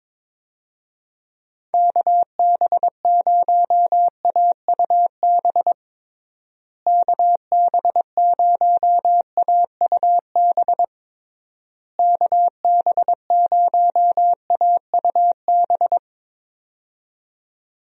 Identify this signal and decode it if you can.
Morse code, heard as a keyed tone, about 22 wpm, 700 Hz. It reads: KB0AUB KB0AUB KB0AUB
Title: USA Call Signs - 6 Characters - Repeated 3x - 22wpm